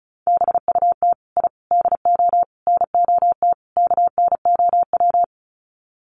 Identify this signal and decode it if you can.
Morse code, heard as a keyed tone, about 34 wpm, 700 Hz. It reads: BUT I DO NOT KNOW
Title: Sets of 5 Words 35wpm